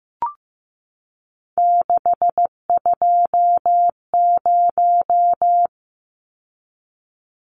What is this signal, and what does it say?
Morse code, heard as a keyed tone, about 15 wpm, 700 Hz. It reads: 620
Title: Numbers - 3-Digit 15wpm